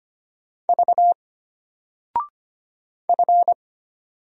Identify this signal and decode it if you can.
Morse code, heard as a keyed tone, about 25 wpm, 700 Hz. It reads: V F
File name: Single-Letter-Number-25wpm